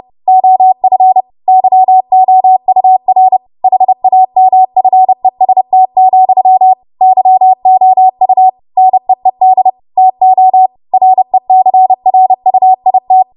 Morse code, heard as a keyed tone, about 30 wpm, 750 Hz. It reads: OF YOUR HAMFEST, YOU NEED TO RECRUIT
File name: Morse / 30WPM